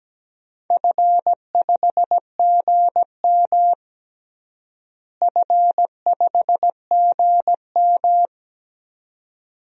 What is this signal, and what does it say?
Morse code, heard as a keyed tone, about 17 wpm, 700 Hz. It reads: F5GM F5GM
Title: DXCC-100 Call Signs - 4 Characters - Repeated 2x - 17wpm